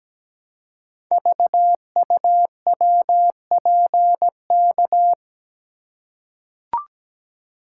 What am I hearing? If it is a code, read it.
Morse code, heard as a keyed tone, about 17 wpm, 700 Hz. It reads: VUWPK
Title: Code Groups - 5 Letters - 17wpm